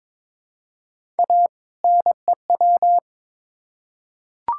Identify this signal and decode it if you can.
Morse code, heard as a keyed tone, about 22 wpm, 700 Hz. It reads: A NEW
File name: Build A Phrase - 2-Word Phrases - 22wpm